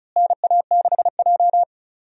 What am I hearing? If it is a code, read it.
Morse code, heard as a keyed tone, about 36 wpm, 700 Hz. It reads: NA6J
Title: USA Call Signs - 4 Characters - Repeated 3x - 35wpm